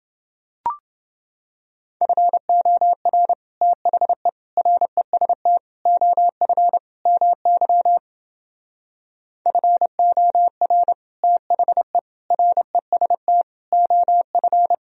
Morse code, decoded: FOR THE REST OF MY FOR THE REST OF

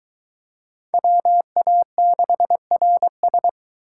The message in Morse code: WA6RS

